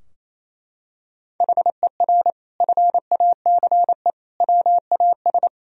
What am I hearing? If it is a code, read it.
Morse code, heard as a keyed tone, about 28 wpm, 700 Hz. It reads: HER FACE WAS
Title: Sentences from Top 1000 Words - Review Part 2 - 28wpm